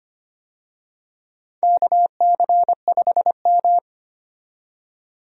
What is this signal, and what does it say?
Morse code, heard as a keyed tone, about 25 wpm, 700 Hz. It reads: KC5M
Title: USA Call Signs - 4 Characters - 25wpm